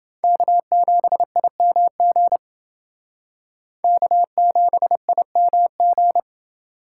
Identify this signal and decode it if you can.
Morse code, heard as a keyed tone, about 28 wpm, 700 Hz. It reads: K7IMG K7IMG